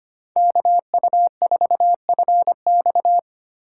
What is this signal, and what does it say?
Morse code, heard as a keyed tone, about 25 wpm, 700 Hz. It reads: KU4FX
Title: USA Call Signs - 6x Character Spacing - 25wpm